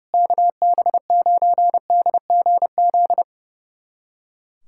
Morse code, heard as a keyed tone, about 30 wpm, 700 Hz. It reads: KB9DGZ
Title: USA Callsign DE Callsign - 6 Characters - 30wpm